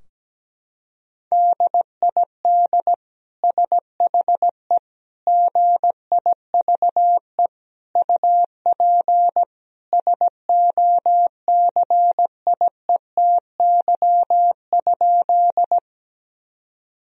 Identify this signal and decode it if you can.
Morse code, heard as a keyed tone, about 17 wpm, 700 Hz. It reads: DID SHE GIVE UP SOCIETY?
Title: Sentences from Top 1000 Words - Review Part 4 - 17wpm